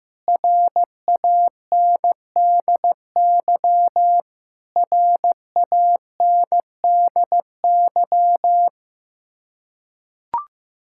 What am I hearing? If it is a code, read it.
Morse code, heard as a keyed tone, about 15 wpm, 700 Hz. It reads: RANDY RANDY
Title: Practice QSO - Exchange 1 - Beginning - 15wpm